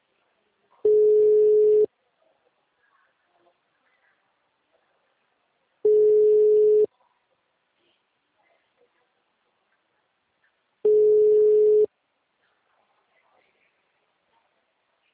Ringback tone (Europe, 425 Hz).